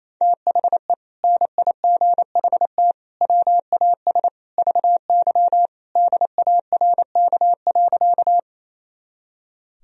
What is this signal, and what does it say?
Morse code, heard as a keyed tone, about 28 wpm, 700 Hz. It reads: THE NIGHT WAS VY DARK.
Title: Sentences from Top 500 Words and Abbreviations 28wpm